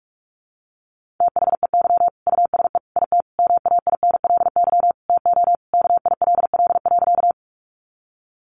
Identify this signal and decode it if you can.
Morse code, heard as a keyed tone, about 44 wpm, 700 Hz. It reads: THEY USE IT MAINLY TO KILL.
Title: Sentences from Top 1400 Words 45wpm